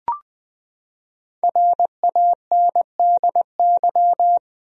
Morse code, heard as a keyed tone, about 20 wpm, 700 Hz. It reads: RANDY
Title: USA Common Names - Max 5 Letters - 20wpm